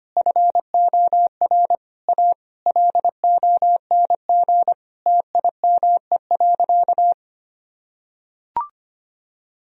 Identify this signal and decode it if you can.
Morse code, heard as a keyed tone, about 25 wpm, 700 Hz. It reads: FOR A LONG TIME.
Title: Sentences from Top 1000 Words - Review Part 2 - 25wpm